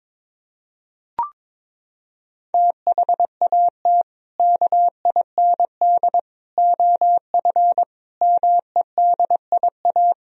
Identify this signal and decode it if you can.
Morse code, heard as a keyed tone, about 22 wpm, 700 Hz. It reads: THAT KIND OF MEDIA
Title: Sentences from Top 1000 Words - Review Part 4 - 22wpm